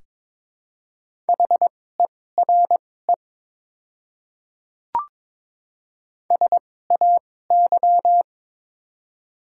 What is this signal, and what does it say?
Morse code, heard as a keyed tone, about 23 wpm, spaced as for 16 wpm, 700 Hz. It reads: HERE SAY
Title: 2x Character Spacing - Top 200 Words - 22wpm